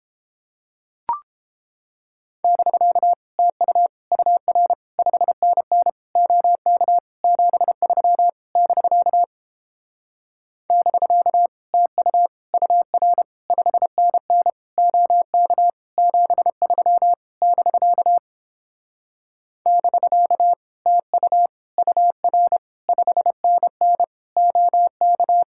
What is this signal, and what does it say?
Morse code, heard as a keyed tone, about 31 wpm, 700 Hz. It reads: <BK> TU UR 5NN OK 73 <BK> <BK> TU UR 5NN OK 73 <BK> <BK> TU UR 5NN OK